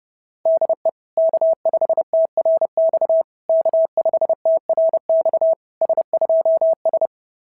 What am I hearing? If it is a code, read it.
Morse code, heard as a keyed tone, about 30 wpm, 690 Hz. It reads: DE K5TRX K5TRX S2S
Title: SOTA QSO Practice 30wpm